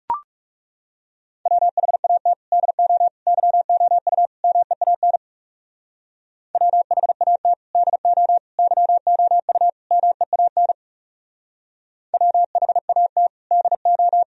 Morse code, heard as a keyed tone, about 37 wpm, 700 Hz. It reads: WHAT DO YOU MEAN WHAT DO YOU MEAN WHAT DO
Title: Speed-Racing - Sets of 4 Words 30wpm